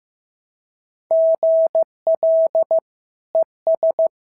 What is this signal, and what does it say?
Morse code, heard as a keyed tone, about 15 wpm, 690 Hz. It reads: GL ES
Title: SOTA QSO Practice 15wpm